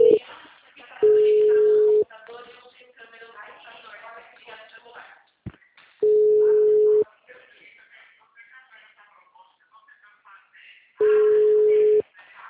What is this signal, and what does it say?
Ringback tone (Europe, 425 Hz), with speech.